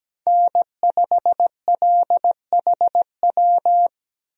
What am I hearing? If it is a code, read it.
Morse code, heard as a keyed tone, about 17 wpm, 700 Hz. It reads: N5LHW
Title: USA Call Signs - 5 Characters - Repeated 3x - 17wpm